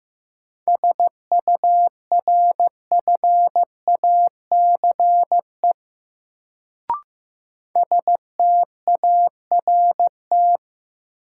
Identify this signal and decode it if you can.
Morse code, heard as a keyed tone, about 15 wpm, 700 Hz. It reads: SURFACE START
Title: Sets of 4 Words - New Word Drill - 15wpm